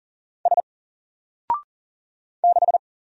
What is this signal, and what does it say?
Morse code, heard as a keyed tone, about 40 wpm, 700 Hz. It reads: S 6
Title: Single Letter-Number 40wpm